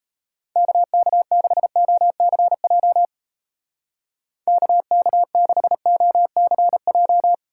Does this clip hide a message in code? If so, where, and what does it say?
Yes: Morse code from 0.6 to 7.3 s, heard as a keyed tone, about 36 wpm, 700 Hz. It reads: KK6OCJ KK6OCJ